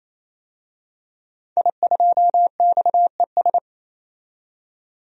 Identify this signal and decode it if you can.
Morse code, heard as a keyed tone, about 28 wpm, 700 Hz. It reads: I2XES
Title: DXCC-100 Call Signs - 5 Characters - 28wpm